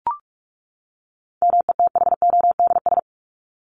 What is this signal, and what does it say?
Morse code, heard as a keyed tone, about 44 wpm, 700 Hz. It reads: METHODS
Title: ES:S Suffix - Top 100 Words with Base Words 45wpm